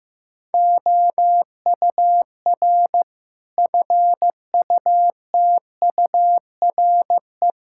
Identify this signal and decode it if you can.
Morse code, heard as a keyed tone, about 15 wpm, 700 Hz. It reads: OUR FUTURE
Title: Sentences from Top 1000 Words - Review Part 5 - 15wpm